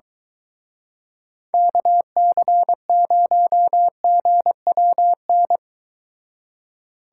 Morse code, heard as a keyed tone, about 24 wpm, 700 Hz. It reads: KC0GWN